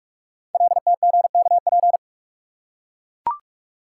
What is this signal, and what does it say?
Morse code, heard as a keyed tone, about 44 wpm, 700 Hz. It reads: LTGKP
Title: Code Groups - 5 Letter:Numbers - 45wpm